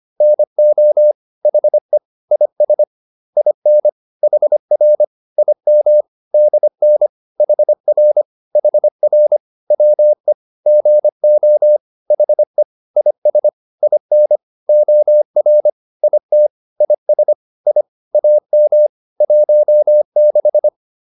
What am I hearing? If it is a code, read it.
Morse code, heard as a keyed tone, about 25 wpm, 600 Hz. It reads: NO HE IS IN HR IM DN HR HR WE GO HE IS IN OR IT IS I AM 16
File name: PR102a 25